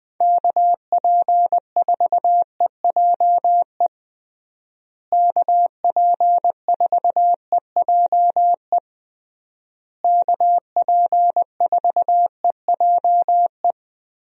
Morse code, heard as a keyed tone, about 20 wpm, 700 Hz. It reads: KP4EJE KP4EJE KP4EJE
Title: DXCC-100 Call Signs - 6 Characters - Repeated 3x - 20wpm